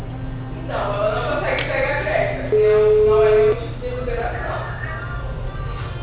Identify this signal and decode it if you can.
Ringback tone (Europe, 425 Hz), with noise.